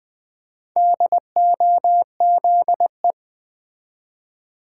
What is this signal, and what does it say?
Morse code, heard as a keyed tone, about 20 wpm, 700 Hz. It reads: DOZE
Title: Build A Word - 1 Letter At A Time - 5-Letter Words - 20wpm